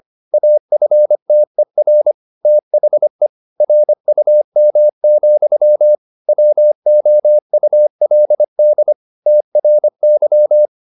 Morse code, heard as a keyed tone, about 25 wpm, 600 Hz. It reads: AFTER THE RUM, WOULD TRY